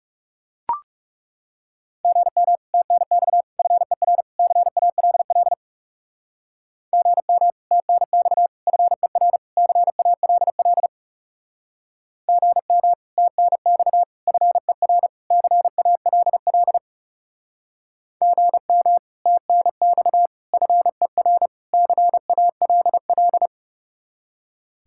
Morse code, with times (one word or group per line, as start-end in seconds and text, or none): 2.0-2.6 GM
2.7-3.4 TNX
3.6-4.2 FER
4.4-5.5 CALL
6.9-7.5 GM
7.7-8.5 TNX
8.7-9.4 FER
9.6-10.9 CALL
12.3-12.9 GM
13.2-14.0 TNX
14.3-15.1 FER
15.3-16.8 CALL
18.2-19.0 GM
19.3-20.3 TNX
20.5-21.5 FER
21.7-23.5 CALL